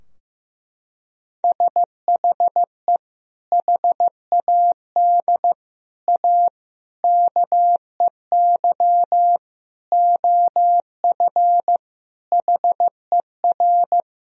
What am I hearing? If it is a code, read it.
Morse code, heard as a keyed tone, about 15 wpm, 700 Hz. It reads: SHE HAD A KEY OF HER
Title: Sentences from Top 1000 Words - Review Part 4 - 15wpm